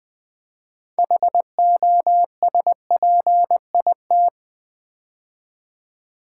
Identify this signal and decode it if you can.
Morse code, heard as a keyed tone, about 20 wpm, 700 Hz. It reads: HOSPIT
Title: Build A Word - 3 Letters At A Time - 8-Letter Words - 20wpm